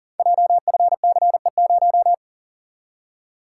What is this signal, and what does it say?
Morse code, heard as a keyed tone, about 42 wpm, 690 Hz. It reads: JFCE0
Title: Code Groups - 5 Letter:Numbers - 40wpm